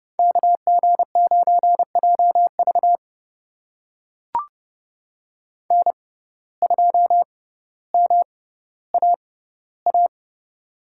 Morse code, heard as a keyed tone, about 30 wpm, spaced as for 15 wpm, 700 Hz. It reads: KG9JV N2MAA
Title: USA Call Signs - 6x Character Spacing - 30wpm